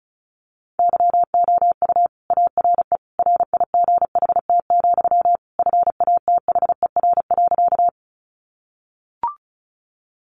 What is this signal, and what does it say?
Morse code, heard as a keyed tone, about 34 wpm, 700 Hz. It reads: YOU ARE RIGHT, FATHER.
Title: Sentences from Top 300 Words 35wpm